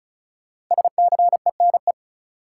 Morse code, heard as a keyed tone, about 34 wpm, 700 Hz. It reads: SCENE
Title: Sentences from Top 1100 Words - New Word Drill Alt - 35wpm